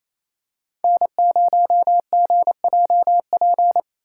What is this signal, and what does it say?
Morse code, heard as a keyed tone, about 28 wpm, 700 Hz. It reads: N0GJP
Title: USA Call Signs - 5 Characters - 2x Character Spacing - 28wpm